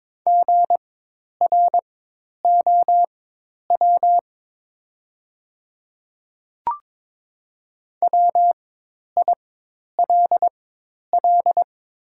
Morse code, heard as a keyed tone, about 22 wpm, spaced as for 10 wpm, 700 Hz. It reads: GROW WILL
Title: 4x Character Spacing - Top 300 Words - 22wpm